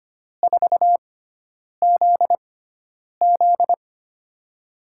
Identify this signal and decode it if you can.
Morse code, heard as a keyed tone, about 25 wpm, spaced as for 9 wpm, 700 Hz. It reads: 4ZZ
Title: USA Call Signs - 6x Character Spacing - 25wpm